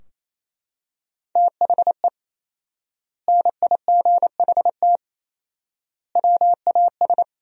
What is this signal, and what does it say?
Morse code, heard as a keyed tone, about 28 wpm, 700 Hz. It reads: THE NIGHT WAS
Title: 4x Word-Spacing - Sentences from Top 400 Words - 28wpm